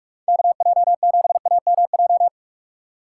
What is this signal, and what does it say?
Morse code, heard as a keyed tone, about 42 wpm, 700 Hz. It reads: KJ7AMJ
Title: USA Call Signs - 2x Character Spacing - 45wpm